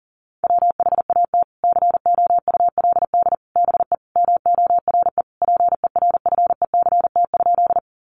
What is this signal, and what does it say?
Morse code, heard as a keyed tone, about 37 wpm, 700 Hz. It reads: WHAT COULD BE MORE PERFECT?
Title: Sentences from Top 1000 Words 40wpm